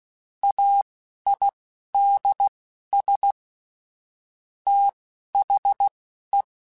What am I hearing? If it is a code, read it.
Morse code, heard as a keyed tone, about 16 wpm, spaced as for 11 wpm, 800 Hz. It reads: AIDS THE
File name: Technical Subjects / Technical Subjects-mp3s / XC @10